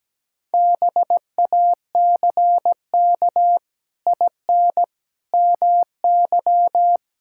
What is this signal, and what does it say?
Morse code, heard as a keyed tone, about 17 wpm, 700 Hz. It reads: BACK IN MY
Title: Pangram Challenge v2 - 17wpm